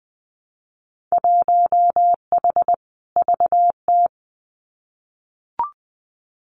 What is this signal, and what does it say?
Morse code, heard as a keyed tone, about 20 wpm, 700 Hz. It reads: 1H VT
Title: Winter Field Day Exchanges 20wpm